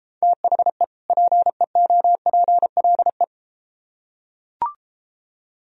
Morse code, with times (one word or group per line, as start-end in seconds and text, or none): 0.2-0.8 THE
1.1-3.2 PEOPLE